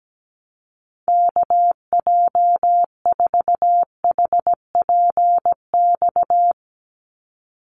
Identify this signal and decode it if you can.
Morse code, heard as a keyed tone, about 17 wpm, 700 Hz. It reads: KJ4HPX
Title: USA Call Signs 17wpm